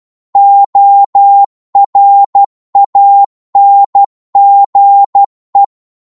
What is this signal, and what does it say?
Morse code, heard as a keyed tone, about 12 wpm, 800 Hz. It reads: ORANGE
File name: cwmp3 / psg-1-12wpm